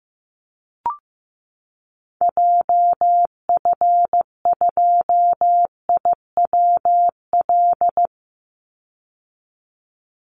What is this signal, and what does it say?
Morse code, heard as a keyed tone, about 14 wpm, 700 Hz. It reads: JF2IWL
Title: Call Signs Repeated 3x - 15wpm